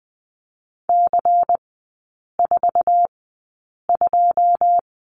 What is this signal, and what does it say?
Morse code, heard as a keyed tone, about 20 wpm, spaced as for 8 wpm, 700 Hz. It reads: C42
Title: 121 Lesson 20e - Number 3 - Three Characters 20wpm